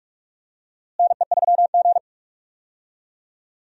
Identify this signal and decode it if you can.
Morse code, heard as a keyed tone, about 46 wpm, 700 Hz. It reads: NE3G
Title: USA Call Signs - 4 Characters - Repeated 3x - 45wpm